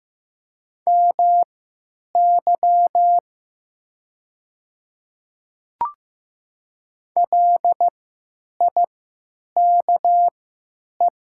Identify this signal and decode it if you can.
Morse code, heard as a keyed tone, about 15 wpm, spaced as for 9 wpm, 700 Hz. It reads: MY LIKE
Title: 3x Character Spacing - Top 100 Words - 15wpm